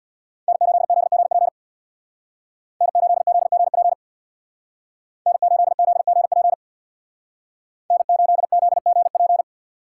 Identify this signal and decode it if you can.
Morse code, heard as a keyed tone, about 59 wpm, 700 Hz. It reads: N8ZGP N8ZGP N8ZGP N8ZGP
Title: Speed Racing - USA 1x3 Call Signs 50wpm